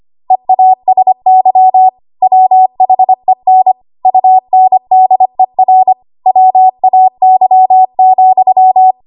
Morse code, heard as a keyed tone, about 25 wpm, 750 Hz. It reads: EASY WHEN UNDER WAY,
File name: Archive / 25 WPM / 25WPM